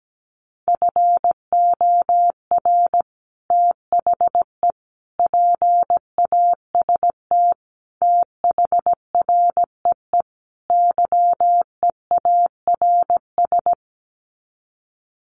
Morse code, heard as a keyed tone, about 17 wpm, 700 Hz. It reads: FOR THE PAST THREE YEARS
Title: Sets of 5 Words Encore2 17wpm